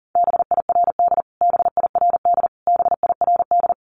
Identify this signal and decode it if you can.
Morse code, heard as a keyed tone, about 40 wpm, 700 Hz. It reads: BIRD BIRD BIRD
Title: Top 500 Words Repeated 40wpm